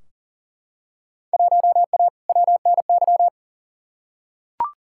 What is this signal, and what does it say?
Morse code, heard as a keyed tone, about 37 wpm, 700 Hz. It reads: 1A WNY
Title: Field Day Exchanges - 40wpm